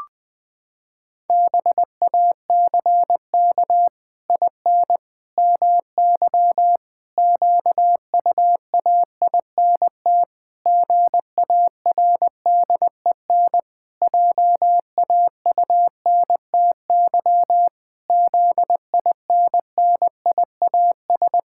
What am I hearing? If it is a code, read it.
Morse code, heard as a keyed tone, about 20 wpm, 700 Hz. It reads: BACK IN MY QUAINT GARDEN JAUNTY ZINNIAS
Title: Pangram Challenge v2 - 20wpm